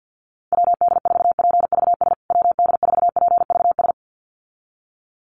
Morse code, heard as a keyed tone, about 50 wpm, 700 Hz. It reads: WD4PVS WD4PVS
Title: USA Call Signs Repeated 2x 50wpm